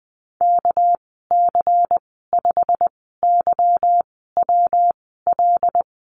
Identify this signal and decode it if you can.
Morse code, heard as a keyed tone, about 20 wpm, 700 Hz. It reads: K C 5 Y W L
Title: 207 Lesson 33g - Letter j - Call Signs 20wpm